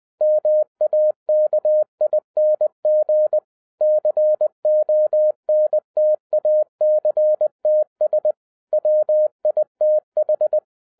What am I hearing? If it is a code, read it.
Morse code, heard as a keyed tone, about 20 wpm, 610 Hz. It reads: MAKING CONTACTS WITH
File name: 2012-01-mp3 / n05-CAME BACK TO HAM RADIO AFTER 30 YEARS OF INTERVAL-20wpm